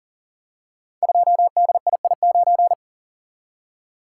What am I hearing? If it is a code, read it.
Morse code, heard as a keyed tone, about 41 wpm, 700 Hz. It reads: 2DII9
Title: Code Groups - 5 Letter:Numbers - 40wpm